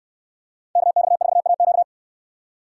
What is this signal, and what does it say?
Morse code, heard as a keyed tone, about 67 wpm, 700 Hz. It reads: KX4AQ